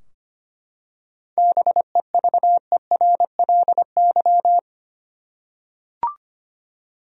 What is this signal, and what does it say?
Morse code, heard as a keyed tone, about 25 wpm, 700 Hz. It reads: BEVERLY
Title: USA Common Names - 25wpm